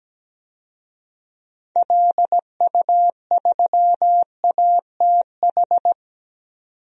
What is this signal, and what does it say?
Morse code, heard as a keyed tone, about 17 wpm, 700 Hz. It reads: LU3ATH
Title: DXCC-100 Call Signs - 6 Characters - 17wpm